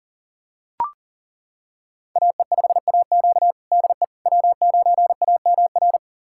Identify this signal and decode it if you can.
Morse code, heard as a keyed tone, about 37 wpm, 700 Hz. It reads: AE5AQ DE W9AMR
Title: USA Callsign DE Callsign - 5 Characters - 40wpm